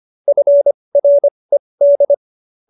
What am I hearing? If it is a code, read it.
Morse code, heard as a keyed tone, about 25 wpm, spaced as for 20 wpm, 570 Hz. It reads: FRED